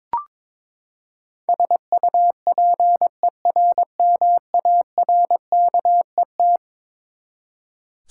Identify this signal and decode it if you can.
Morse code, heard as a keyed tone, about 22 wpm, 700 Hz. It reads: SUPERMARKET
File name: practice / 22wpm / Eleven Letter Words - 22wpm